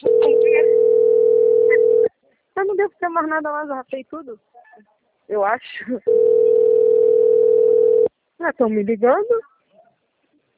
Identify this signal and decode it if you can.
Ringback tone (North America), 440 + 480 Hz, with speech.
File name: out- - - - - .49874